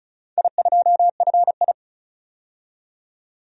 Morse code, heard as a keyed tone, about 36 wpm, 700 Hz. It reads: I2FI
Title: DXCC-100 Call Signs - 4 Characters - 35wpm